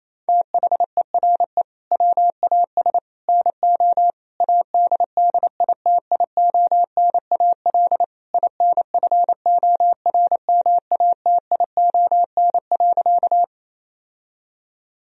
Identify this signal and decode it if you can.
Morse code, heard as a keyed tone, about 27 wpm, 700 Hz. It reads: THERE WAS NO ADDITIONAL INFORMATION.
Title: Sentences from Top 1000 Words - Review Part 4 - 28wpm